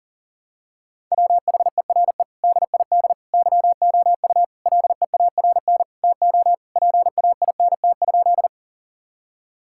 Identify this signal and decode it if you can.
Morse code, heard as a keyed tone, about 38 wpm, 700 Hz. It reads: WHERE DID YOU LEARN TO PAINT?
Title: Sentences from Top 1000 Words - Review Part 6 - 40wpm